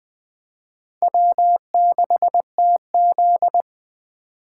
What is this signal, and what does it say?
Morse code, heard as a keyed tone, about 23 wpm, 700 Hz. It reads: W6TZ
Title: Speed Racing - USA 1x2 Call Signs 20wpm